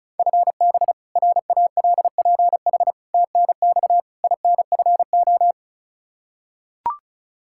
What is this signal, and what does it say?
Morse code, heard as a keyed tone, about 36 wpm, 700 Hz. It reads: FB RALPH TNX INFO
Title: Practice QSO - Exchange 3 - More Info - 35wpm